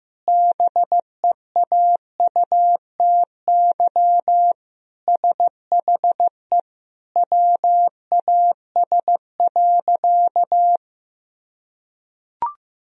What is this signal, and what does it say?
Morse code, heard as a keyed tone, about 16 wpm, 700 Hz. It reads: BEAUTY SHE WAS.
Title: Sentences from Top 1000 Words - Review Part 2 - 15wpm